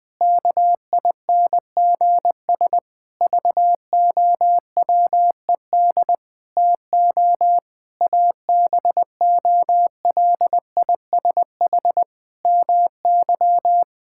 Morse code, heard as a keyed tone, about 20 wpm, 700 Hz. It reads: KINGS VOWED TO ABOLISH MY
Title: Pangram Challenge v2 - 20wpm